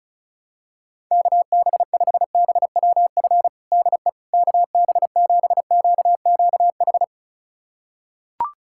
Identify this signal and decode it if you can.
Morse code, heard as a keyed tone, about 36 wpm, 700 Hz. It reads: KB5BWF DE KB7QQH
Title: USA Callsign DE Callsign 35wpm